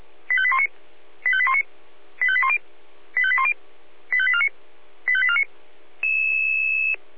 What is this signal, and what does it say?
Five-tone selective call (ZVEI), heard as quick runs of five tones, tones of about 70 ms. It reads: 85819 85819 85810 85810 85849 85849